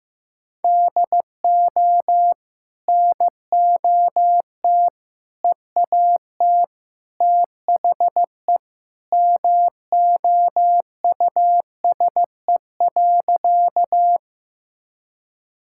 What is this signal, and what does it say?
Morse code, heard as a keyed tone, about 15 wpm, 700 Hz. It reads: DO NOT EAT THE MOUSE.
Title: Sentences from Top 1000 Words - Review Part 5 - 15wpm